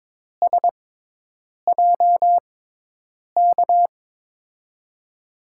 Morse code, heard as a keyed tone, about 22 wpm, 700 Hz. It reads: S J K
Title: USA Call Signs - 6 Characters - 6x Character Spacing - 22wpm